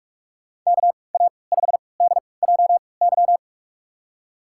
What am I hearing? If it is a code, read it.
Morse code, heard as a keyed tone, about 45 wpm, spaced as for 25 wpm, 700 Hz. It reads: KA5DJY
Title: USA Call Signs - 3x Character Spacing - 45wpm